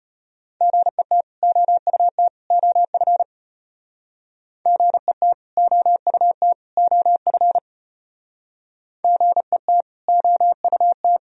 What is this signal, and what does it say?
Morse code, heard as a keyed tone, about 30 wpm, 700 Hz. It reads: GET OUT OF GET OUT OF GET OUT
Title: Speed-Racing - Sets of 3 Words 25wpm